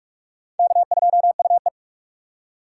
Morse code, heard as a keyed tone, about 44 wpm, 700 Hz. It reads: K2UE